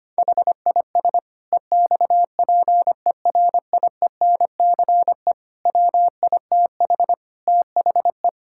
Morse code, heard as a keyed tone, about 25 wpm, 700 Hz. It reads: HIS EXPERIENCE WITH THE
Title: Sentences from Top 1000 Words - Review Part 6 - 25wpm